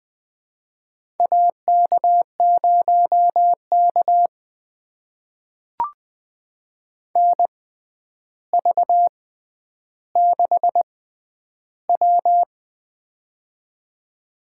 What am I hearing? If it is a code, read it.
Morse code, heard as a keyed tone, about 20 wpm, spaced as for 10 wpm, 700 Hz. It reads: AK0K NV6W
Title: USA Call Signs - 4 Characters - 6x Character Spacing - 20wpm